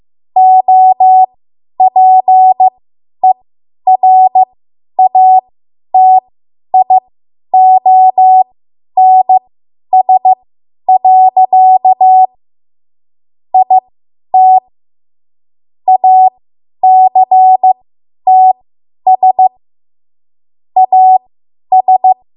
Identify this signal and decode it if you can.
Morse code, heard as a keyed tone, about 15 wpm, spaced as for 10 wpm, 750 Hz. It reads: OPERATIONS. IT ACTS AS
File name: Morse / 10WPM